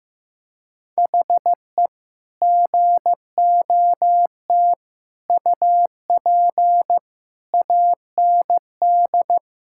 Morse code, heard as a keyed tone, about 16 wpm, 700 Hz. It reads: HE GOT UP AND
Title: Sentences from Top 1600 Words 15wpm